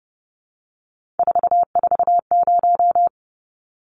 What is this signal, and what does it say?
Morse code, heard as a keyed tone, about 30 wpm, 700 Hz. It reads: 440
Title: Numbers - 3-Digit 30wpm